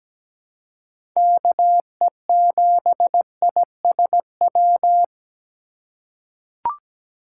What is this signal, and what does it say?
Morse code, heard as a keyed tone, about 17 wpm, 700 Hz. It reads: KE7ISW
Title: USA Call Signs - 6 Characters - 3x Character Spacing - 17wpm